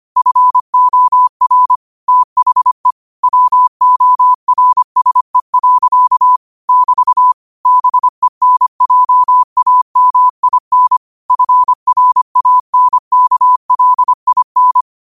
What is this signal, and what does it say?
Morse code, heard as a keyed tone, about 25 wpm, 1000 Hz. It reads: FOR THE WORSE. <BT> BENJAMIN FRANKLIN
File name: qotdfiles / 25wpm